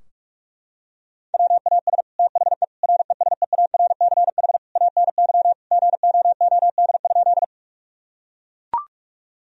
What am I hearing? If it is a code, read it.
Morse code, heard as a keyed tone, about 41 wpm, 700 Hz. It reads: WAS THE RESEARCH ANY GOOD?
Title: Sentences from Top 1000 Words - Review Part 4 - 45wpm